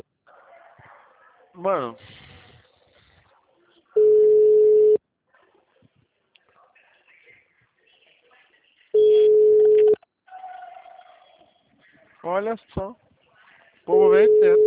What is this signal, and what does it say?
Ringback tone (Europe, 425 Hz), with speech.